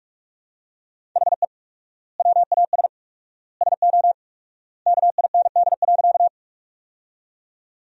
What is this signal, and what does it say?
Morse code, heard as a keyed tone, about 42 wpm, 700 Hz. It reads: HE WAS SO KIND.
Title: 4x Word-Spacing - Sentences from Top 200 Words - 45wpm